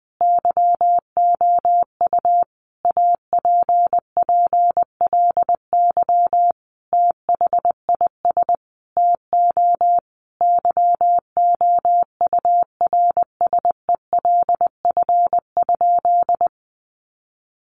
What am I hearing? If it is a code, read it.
Morse code, heard as a keyed tone, about 20 wpm, 700 Hz. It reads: YOU APPLY THIS TO YOURSELF?
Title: Sentences from Top 1200 Words 20wpm